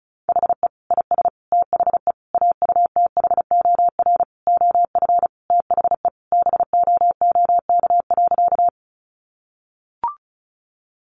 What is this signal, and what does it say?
Morse code, heard as a keyed tone, about 34 wpm, 700 Hz. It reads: HE IS THE AUTHOR OF THE BOOK.
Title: Sentences from Top 1200 Words 35wpm